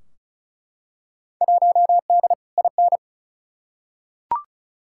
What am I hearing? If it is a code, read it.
Morse code, heard as a keyed tone, about 36 wpm, 700 Hz. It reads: 1D IN
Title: Field Day Exchanges Encore - 35wpm